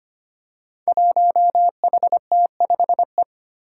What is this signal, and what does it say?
Morse code, heard as a keyed tone, about 25 wpm, 700 Hz. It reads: 1HT5E
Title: Five Challenging Letter-Number Combinations 25wpm